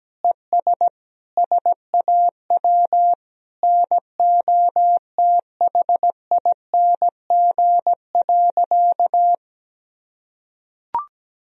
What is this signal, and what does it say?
Morse code, heard as a keyed tone, about 17 wpm, 700 Hz. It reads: ES SAW NOTHING.
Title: Sentences from Top 500 Words and Abbreviations 17wpm